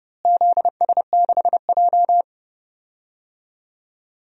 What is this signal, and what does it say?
Morse code, heard as a keyed tone, about 30 wpm, 700 Hz. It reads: ZS6J
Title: DXCC-100 Call Signs - 4 Characters - Repeated 3x - 30wpm